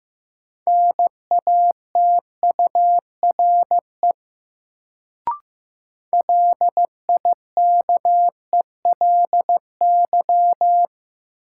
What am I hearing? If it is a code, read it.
Morse code, heard as a keyed tone, about 15 wpm, 700 Hz. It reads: NATURE LIKELY
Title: Sentences from Top 600 Words - New Word Drill - 15wpm